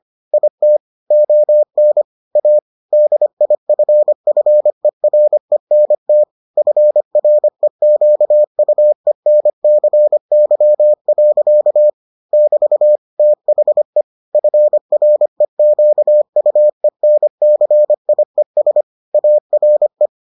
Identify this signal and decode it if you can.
Morse code, heard as a keyed tone, about 25 wpm, 600 Hz. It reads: IT ON A DIFFERENT FREQUENCY. <BT> THE FREQUENCIES ARE